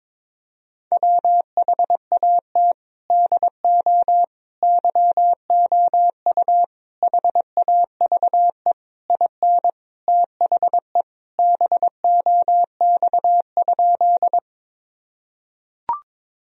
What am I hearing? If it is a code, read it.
Morse code, heard as a keyed tone, about 22 wpm, 700 Hz. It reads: WHAT DO YOU HAVE IN THE BOX?
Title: Sentences from Top 1000 Words - Review Part 2 - 22wpm